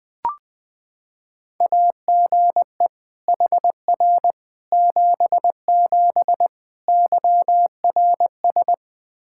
Morse code, heard as a keyed tone, about 20 wpm, 700 Hz. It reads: AGE HR 77 YRS
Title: Practice QSO - Exchange 3 - More Info - 20wpm